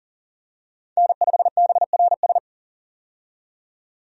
Morse code, heard as a keyed tone, about 40 wpm, 700 Hz. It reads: N5BRS
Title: USA Call Signs - 5 Characters - 40wpm